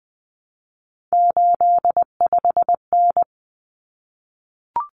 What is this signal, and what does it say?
Morse code, heard as a keyed tone, about 20 wpm, 700 Hz. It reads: 85N
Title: Common Cut Numbers - 3-Digit 20wpm